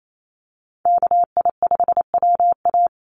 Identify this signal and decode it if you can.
Morse code, heard as a keyed tone, about 28 wpm, 700 Hz. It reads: KI5WA
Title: NA SOTA Chaser Callsigns 28wpm